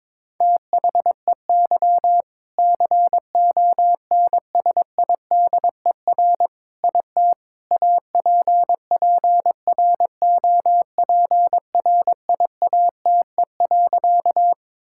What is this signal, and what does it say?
Morse code, heard as a keyed tone, about 22 wpm, 700 Hz. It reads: THEY CONSIDER IT APPROPRIATE.
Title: Sentences from Top 1000 Words - Review Part 5 - 22wpm